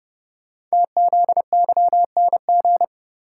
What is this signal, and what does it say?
Morse code, heard as a keyed tone, about 29 wpm, 700 Hz. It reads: TZYNG
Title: Code Groups - 5 Letter:Numbers - 30wpm